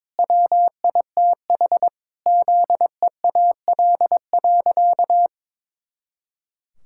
Morse code, heard as a keyed tone, about 22 wpm, 700 Hz. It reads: WITH ZEAL.
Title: Pangram Challenge v2 - 22wpm